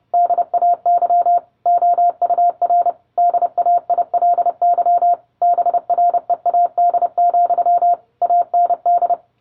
Morse code, heard as a keyed tone, about 30 wpm, 700 Hz. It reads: DAY OUR DAILY BREAD, AND